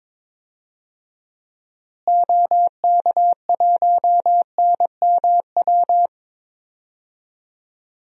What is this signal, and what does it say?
Morse code, heard as a keyed tone, about 22 wpm, 700 Hz. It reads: OK1NMW
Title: DXCC-100 Call Signs - 6 Characters - Repeated 3x - 22wpm